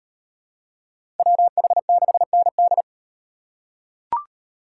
Morse code, heard as a keyed tone, about 34 wpm, 700 Hz. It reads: WH6ND